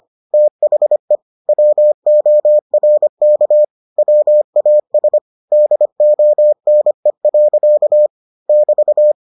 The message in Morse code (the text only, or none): THE WORK WAS DONE. <BT>